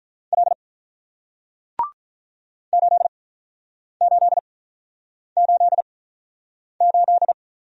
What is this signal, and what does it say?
Morse code, heard as a keyed tone, about 48 wpm, spaced as for 9 wpm, 700 Hz. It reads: L 8888